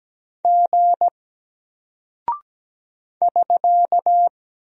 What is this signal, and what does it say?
Morse code, heard as a keyed tone, about 17 wpm, 700 Hz. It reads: G <SK>
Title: Single Character (Letter:Number:Punctuation:Prosign) Alt 17wpm